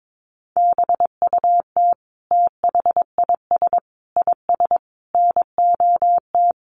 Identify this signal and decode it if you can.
Morse code, heard as a keyed tone, about 22 wpm, 700 Hz. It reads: BUT THIS IS NOT
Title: Sentences from Top 1400 Words 22wpm